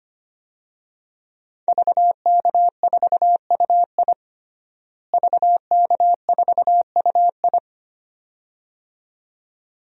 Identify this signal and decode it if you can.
Morse code, heard as a keyed tone, about 25 wpm, 700 Hz. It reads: VK4UI VK4UI
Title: DXCC-100 Call Signs - 5 Characters - Repeated 2x - 25wpm